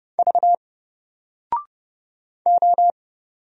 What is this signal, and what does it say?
Morse code, heard as a keyed tone, about 30 wpm, 700 Hz. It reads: V O
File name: Single-Letter-Number-30wpm